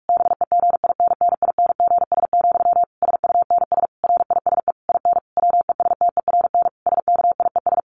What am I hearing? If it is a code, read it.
Morse code, heard as a keyed tone, about 46 wpm, 700 Hz. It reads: BEGINNINGS, SUNS RISE IN WESTERN SKIES
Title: A Sound Of Thunder 45wpm